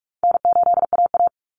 Morse code, heard as a keyed tone, about 44 wpm, 700 Hz. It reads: N8AA
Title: Easy Call Signs Repeated 3x - 45wpm